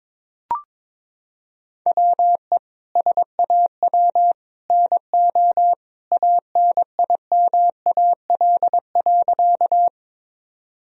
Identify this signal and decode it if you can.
Morse code, heard as a keyed tone, about 22 wpm, 700 Hz. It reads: WE SAW NO ANIMAL.
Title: Sentences from Top 1000 Words - Review Part 1 - 22wpm